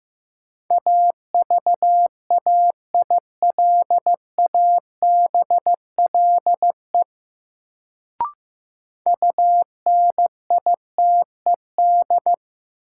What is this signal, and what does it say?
Morse code, heard as a keyed tone, about 15 wpm, 700 Hz. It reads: AVAILABLE UNITED
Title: Sentences from Top 600 Words - New Word Drill - 15wpm